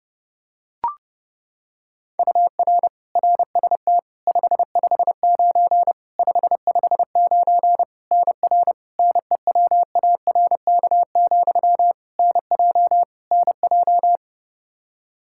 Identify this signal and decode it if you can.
Morse code, heard as a keyed tone, about 30 wpm, 700 Hz. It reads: UR RST 559 559 NR NEWARK, NJ NJ
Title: K1USN SST QSO Practice 30wpm